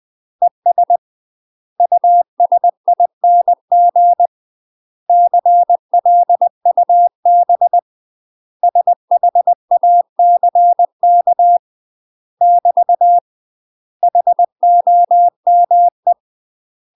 Morse code, heard as a keyed tone, about 20 wpm, 700 Hz. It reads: ES USING CLUB SHACK <BT> HOME
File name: UK MASTER QSOs TRAINING / 20WPM / UK MASTER QSOs 0063